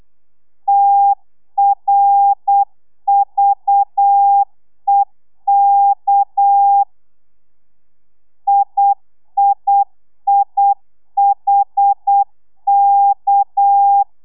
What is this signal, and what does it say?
Morse code, heard as a keyed tone, about 8 wpm, 800 Hz. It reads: TRVEK IIIHK
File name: telegrafia / i2tfj / Lez6 35